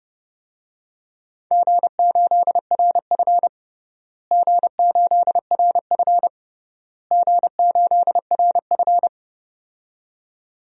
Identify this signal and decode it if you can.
Morse code, heard as a keyed tone, about 30 wpm, 700 Hz. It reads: G8RF G8RF G8RF
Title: DXCC-100 Call Signs - 4 Characters - Repeated 3x - 30wpm